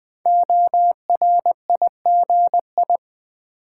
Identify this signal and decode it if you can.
Morse code, heard as a keyed tone, about 20 wpm, 700 Hz. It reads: ORIGI
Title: Build A Word - 5 Letters At A Time - 10-Letter Words - 20wpm